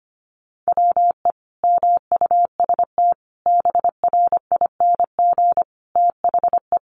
Morse code, heard as a keyed tone, about 25 wpm, 700 Hz. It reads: WE MUST BRING THE
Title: Sentences from Top 1200 Words 25wpm